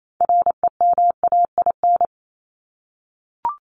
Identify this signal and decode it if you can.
Morse code, heard as a keyed tone, about 28 wpm, 700 Hz. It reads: REMAIN
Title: ES:S Suffix - Top 100 Words with Base Words 28wpm